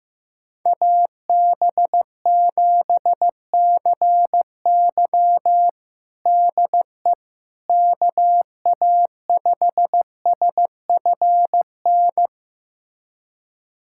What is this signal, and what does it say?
Morse code, heard as a keyed tone, about 15 wpm, 700 Hz. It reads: AB7CY DE KA5SFN
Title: USA Callsign DE Callsign 15wpm